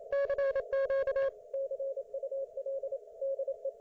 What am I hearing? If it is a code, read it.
Morse code, heard as a keyed tone, about 28 wpm, 560 Hz. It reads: CQ CUL DE